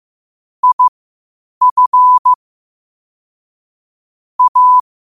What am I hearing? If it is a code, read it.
Morse code, heard as a keyed tone, about 15 wpm, spaced as for 8 wpm, 1000 Hz. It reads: IF A
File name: qotdfiles / 8wpm